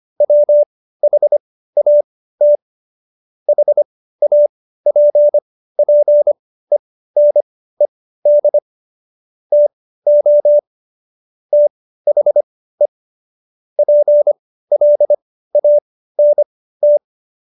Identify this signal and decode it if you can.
Morse code, heard as a keyed tone, about 25 wpm, spaced as for 15 wpm, 590 Hz. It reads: WHAT HAPPENED TO THE PLANT
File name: Jokes-25-char-15-eff